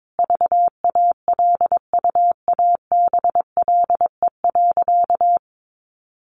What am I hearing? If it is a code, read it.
Morse code, heard as a keyed tone, about 22 wpm, 700 Hz. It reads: VALUABLE.
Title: Sentences from Top 1200 Words 22wpm